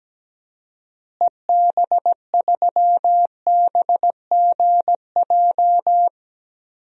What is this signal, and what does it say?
Morse code, heard as a keyed tone, about 17 wpm, 700 Hz. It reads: EB3BGJ
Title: DXCC-100 Call Signs - 6 Characters - Repeated 3x - 17wpm